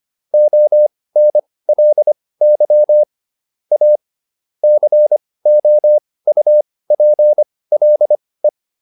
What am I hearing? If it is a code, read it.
Morse code, heard as a keyed tone, about 25 wpm, spaced as for 18 wpm, 600 Hz. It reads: ONLY A COUPLE